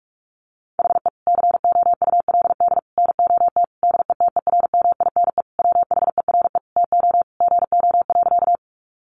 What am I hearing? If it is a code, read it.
Morse code, heard as a keyed tone, about 44 wpm, 700 Hz. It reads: HE COULD NOT DETERMINE WHERE TO GO.
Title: Sentences from Top 700 Words 45wpm